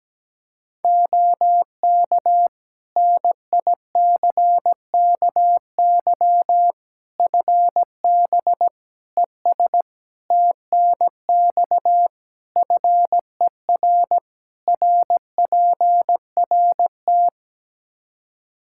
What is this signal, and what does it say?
Morse code, heard as a keyed tone, about 17 wpm, 700 Hz. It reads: OK NICKY FB ES TNX FER RPRT
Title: Practice QSO - Exchange 2 - Info - 17wpm